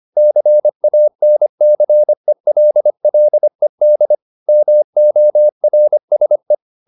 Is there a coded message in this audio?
Yes: Morse code from 0.2 to 6.5 s, heard as a keyed tone, about 25 wpm, 600 Hz. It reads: CANCELLED MORSE